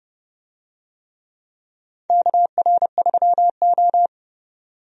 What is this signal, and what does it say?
Morse code, heard as a keyed tone, about 30 wpm, 700 Hz. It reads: KR3O
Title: USA Call Signs - 4 Characters - 30wpm